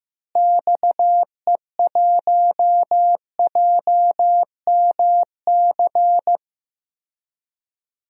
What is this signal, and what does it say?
Morse code, heard as a keyed tone, about 15 wpm, 700 Hz. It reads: XE1JMC
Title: DXCC-100 Call Signs - 6 Characters - Repeated 2x - 15wpm